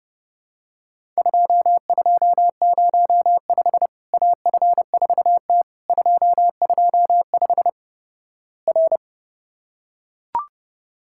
Morse code, heard as a keyed tone, about 30 wpm, 700 Hz. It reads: 2205 AF4T 225 R
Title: Worked All Europe DX Contest - QTC Exchange - 30wpm